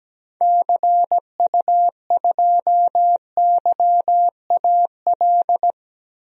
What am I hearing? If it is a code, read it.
Morse code, heard as a keyed tone, about 17 wpm, 700 Hz. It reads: CU2YAL
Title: DXCC-100 Call Signs - 6 Characters - Repeated 2x - 17wpm